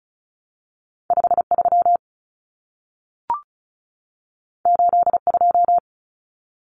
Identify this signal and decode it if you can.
Morse code, heard as a keyed tone, about 36 wpm, 700 Hz. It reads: 53 82
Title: Common Cut Numbers - 2-Digit 35wpm